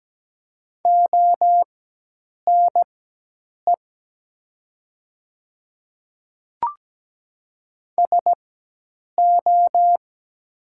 Morse code, heard as a keyed tone, about 17 wpm, spaced as for 8 wpm, 700 Hz. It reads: ONE SO
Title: 4x Character Spacing - Top 100 Words - 17wpm